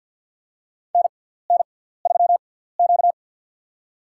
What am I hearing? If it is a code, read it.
Morse code, heard as a keyed tone, about 53 wpm, spaced as for 17 wpm, 700 Hz. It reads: NN3Q
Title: USA Call Signs - 4 Characters - 6x Character Spacing - 50wpm